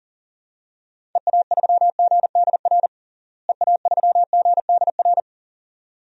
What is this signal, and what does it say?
Morse code, heard as a keyed tone, about 40 wpm, 700 Hz. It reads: EA3GDR EA3GDR
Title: DXCC-100 Call Signs - 6 Characters - Repeated 2x - 40wpm